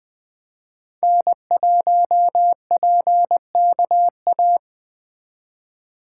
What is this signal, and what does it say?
Morse code, heard as a keyed tone, about 20 wpm, 700 Hz. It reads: N1PKA
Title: Speed Racing - USA Call Signs 17wpm